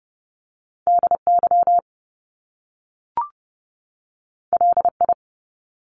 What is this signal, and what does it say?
Morse code, heard as a keyed tone, about 30 wpm, 700 Hz. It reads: DY LI
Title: Top Two Letter Combinations in Words 30wpm